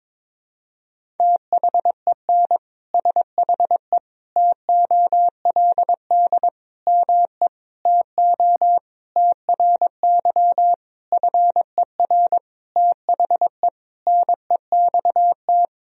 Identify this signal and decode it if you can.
Morse code, heard as a keyed tone, about 22 wpm, 700 Hz. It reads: THEN SHE TOLD ME TO TRY FER THE NEXT
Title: Sentences from Top 400 Words and Abbreviations 22wpm